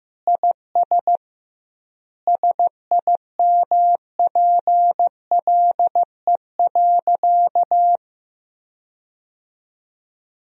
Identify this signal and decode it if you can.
Morse code, heard as a keyed tone, about 15 wpm, 700 Hz. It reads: IS SIMPLE.
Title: 2x Word-Spacing - Sentences from Top 400 Words - 15wpm